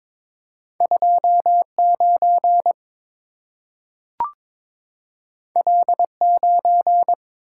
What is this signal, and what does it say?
Morse code, heard as a keyed tone, about 23 wpm, 700 Hz. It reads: 29 L9
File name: practice / 22wpm / Two Letter-Number Combinations - 22wpm